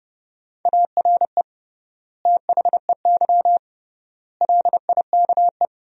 Morse code, heard as a keyed tone, about 29 wpm, 700 Hz. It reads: ARE THEY LIKE
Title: 3x Word-Spacing - Sentences from Top 200 Words - 30wpm